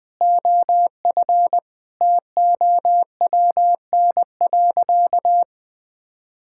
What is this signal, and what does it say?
Morse code, heard as a keyed tone, about 20 wpm, 700 Hz. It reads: OF TOWN.
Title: Sentences with 3-Word Phrases - 20wpm